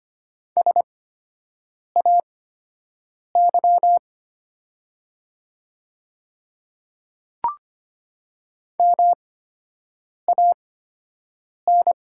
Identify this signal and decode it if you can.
Morse code, heard as a keyed tone, about 25 wpm, spaced as for 7 wpm, 700 Hz. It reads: SAY MAN
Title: 8x Character Spacing - Top 300 Words - 25wpm